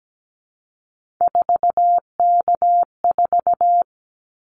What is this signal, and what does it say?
Morse code, heard as a keyed tone, about 17 wpm, 700 Hz. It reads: 4K4
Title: Callsign Prefixes for Top 100 DXCC - 17wpm